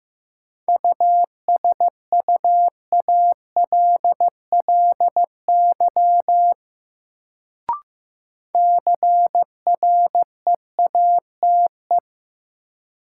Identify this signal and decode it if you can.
Morse code, heard as a keyed tone, about 15 wpm, 700 Hz. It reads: USUALLY CREATE
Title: Sentences from Top 1000 Words - Review Part 3 - New Word Drill - 15wpm